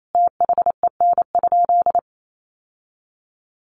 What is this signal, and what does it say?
Morse code, heard as a keyed tone, about 28 wpm, 700 Hz. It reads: THEN?
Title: Sentences from Top 1400 Words 28wpm